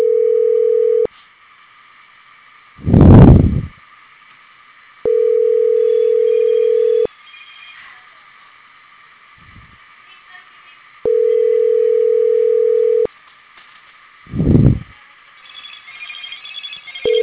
Ringback tone (North America), 440 + 480 Hz.